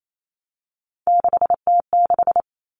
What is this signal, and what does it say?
Morse code, heard as a keyed tone, about 28 wpm, 700 Hz. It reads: 6T6
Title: All Cut Numbers - 3-Digit 28wpm